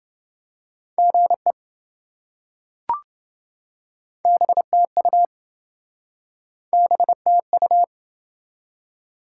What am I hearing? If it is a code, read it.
Morse code, heard as a keyed tone, about 26 wpm, 700 Hz. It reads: GE BTU BTU